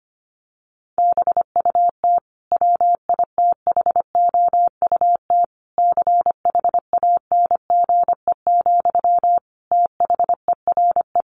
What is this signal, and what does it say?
Morse code, heard as a keyed tone, about 25 wpm, 700 Hz. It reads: BUT WITHOUT CHANGE, THERE
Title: Sentences from Top 1400 Words 25wpm